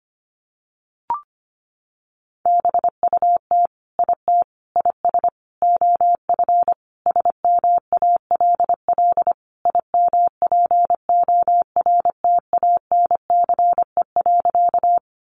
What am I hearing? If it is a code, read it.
Morse code, heard as a keyed tone, about 25 wpm, 700 Hz. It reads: BUT IT IS OF SMALL IMPORTANCE.
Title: Sentences from Top 1200 Words 25wpm